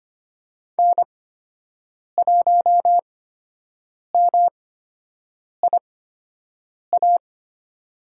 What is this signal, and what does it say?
Morse code, heard as a keyed tone, about 25 wpm, spaced as for 7 wpm, 700 Hz. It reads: N1MIA
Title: USA Call Signs - 5 Characters - 8x Character Spacing - 25wpm